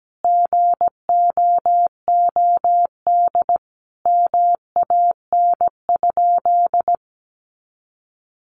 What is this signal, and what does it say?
Morse code, heard as a keyed tone, about 17 wpm, 700 Hz. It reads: GOOD MAN?
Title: Sentences from Top 200 Words 17wpm